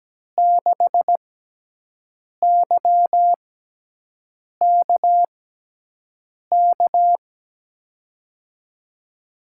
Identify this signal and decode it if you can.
Morse code, heard as a keyed tone, about 17 wpm, spaced as for 6 wpm, 700 Hz. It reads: 6YKK
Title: USA Call Signs - 6 Characters - 6x Character Spacing - 17wpm